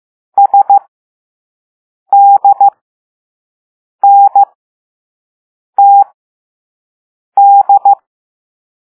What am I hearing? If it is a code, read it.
Morse code, heard as a keyed tone, about 15 wpm, spaced as for 5 wpm, 800 Hz. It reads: SDNTD